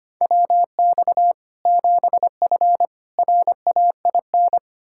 Morse code, heard as a keyed tone, about 25 wpm, 700 Hz. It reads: WX 7F RAIN
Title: Practice QSO - Exchange 2 - Info - 25wpm